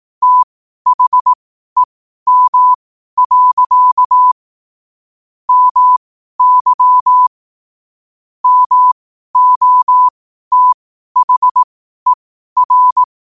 Morse code, heard as a keyed tone, about 18 wpm, spaced as for 12 wpm, 1000 Hz. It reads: THEM. MY MOTHER
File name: qotdfiles / 12wpm